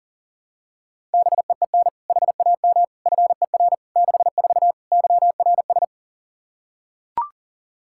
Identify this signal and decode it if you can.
Morse code, heard as a keyed tone, about 40 wpm, 700 Hz. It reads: BEEN HAM FER 64 YRS
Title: K1USN SST QSO Practice 40wpm